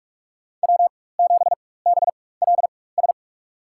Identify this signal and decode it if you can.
Morse code, heard as a keyed tone, about 46 wpm, spaced as for 21 wpm, 700 Hz. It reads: W7BLS
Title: USA Call Signs - 5 Characters - 4x Character Spacing - 45wpm